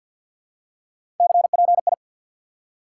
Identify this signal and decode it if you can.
Morse code, heard as a keyed tone, about 49 wpm, 700 Hz. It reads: CPI
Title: Top 90 QSO Elements 50wpm